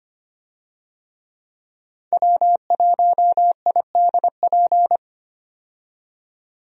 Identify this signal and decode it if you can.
Morse code, heard as a keyed tone, about 25 wpm, 700 Hz. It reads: W1IDP
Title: USA Call Signs - 5 Characters - 25wpm